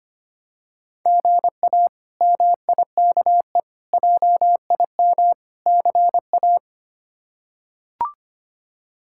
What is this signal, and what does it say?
Morse code, heard as a keyed tone, about 25 wpm, 700 Hz. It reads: GA MIKE JIM CA
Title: K1USN SST QSO Practice - 25wpm